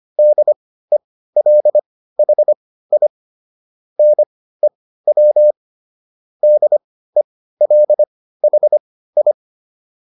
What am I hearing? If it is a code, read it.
Morse code, heard as a keyed tone, about 25 wpm, spaced as for 15 wpm, 600 Hz. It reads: DELHI NEW DELHI